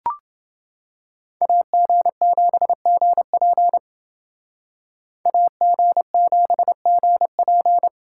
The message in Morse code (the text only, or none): AG7GP AG7GP